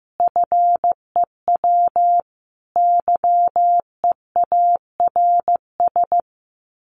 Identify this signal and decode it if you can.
Morse code, heard as a keyed tone, about 15 wpm, 700 Hz. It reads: FEW YEARS
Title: Sets of 5 Words 15wpm